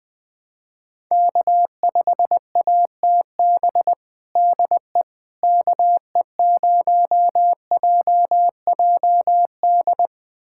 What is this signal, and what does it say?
Morse code, heard as a keyed tone, about 20 wpm, 700 Hz. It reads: K5ATB DE KE0JJD
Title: USA Callsign DE Callsign 20wpm